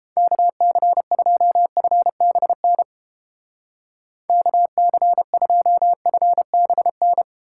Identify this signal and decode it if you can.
Morse code, heard as a keyed tone, about 29 wpm, 700 Hz. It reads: KC2FBN KC2FBN